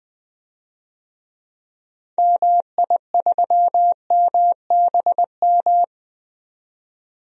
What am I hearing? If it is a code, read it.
Morse code, heard as a keyed tone, about 20 wpm, 700 Hz. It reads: MI3MBM
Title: DXCC-100 Call Signs - 6 Characters - 20wpm